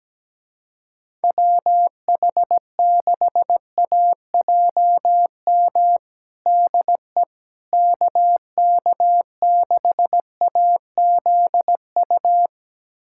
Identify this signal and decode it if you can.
Morse code, heard as a keyed tone, about 17 wpm, 700 Hz. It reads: WH6AJM DE KK6AZU
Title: USA Callsign DE Callsign 17wpm